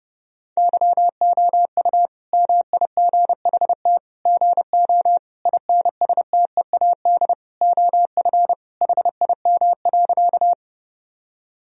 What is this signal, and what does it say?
Morse code, heard as a keyed tone, about 30 wpm, 700 Hz. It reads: YOU MIGHT GO INSTEAD OF HIM.
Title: Sentences from Top 1000 Words - Review Part 4 - 30wpm